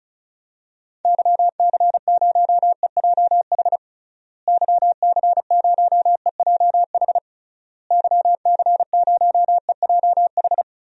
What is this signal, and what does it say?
Morse code, heard as a keyed tone, about 34 wpm, 700 Hz. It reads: YC0EJH YC0EJH YC0EJH
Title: DXCC-100 Call Signs - 6 Characters - Repeated 3x - 35wpm